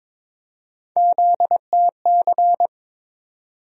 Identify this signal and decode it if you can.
Morse code, heard as a keyed tone, about 21 wpm, 700 Hz. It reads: ZTC
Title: USA Call Sign Postfixes - 3 Letters 22wpm